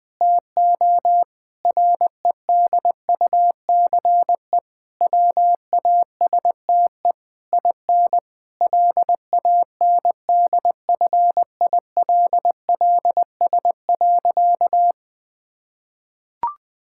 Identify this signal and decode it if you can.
Morse code, heard as a keyed tone, about 20 wpm, 700 Hz. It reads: TO REDUCE WASTE IN LANDFILLS.
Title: Difficult Sentences with Hints - 20wpm